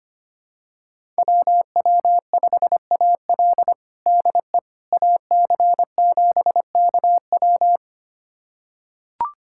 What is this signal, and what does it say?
Morse code, heard as a keyed tone, about 25 wpm, 700 Hz. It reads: WW5AL DE AC7KW
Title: USA Callsign DE Callsign - 5 Characters - 25wpm